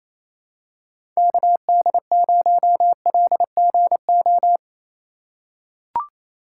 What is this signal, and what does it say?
Morse code, heard as a keyed tone, about 28 wpm, 700 Hz. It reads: KD0LGO
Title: USA Call Signs - 6 Characters - 4x Character Spacing - 28wpm